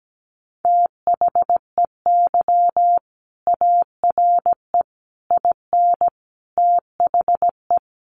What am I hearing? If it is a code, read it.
Morse code, heard as a keyed tone, about 17 wpm, 700 Hz. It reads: THEY ARE IN THE
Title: Sentences from Top 600 Words 17wpm